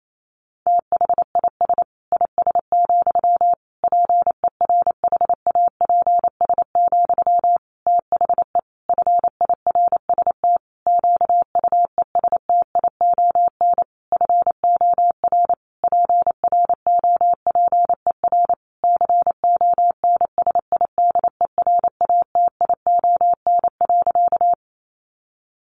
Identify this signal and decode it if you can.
Morse code, heard as a keyed tone, about 28 wpm, 700 Hz. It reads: THIS IS, PERHAPS, THE FIRST QUESTION FOR PROPER CONSIDERATION.
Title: Sentences from Top 1500 Words 28wpm